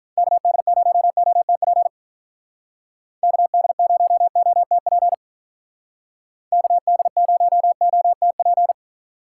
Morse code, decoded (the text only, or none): KD0OTP KD0OTP KD0OTP